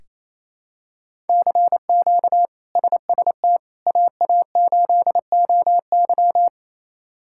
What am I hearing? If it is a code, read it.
Morse code, heard as a keyed tone, about 28 wpm, 700 Hz. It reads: CQ SST AA8OY
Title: K1USN SST QSO Practice - 28wpm